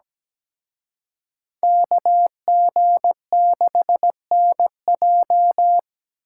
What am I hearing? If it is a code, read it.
Morse code, heard as a keyed tone, about 20 wpm, 700 Hz. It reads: KG6NJ
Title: Speed Racing - USA 2x2 Call Signs 17wpm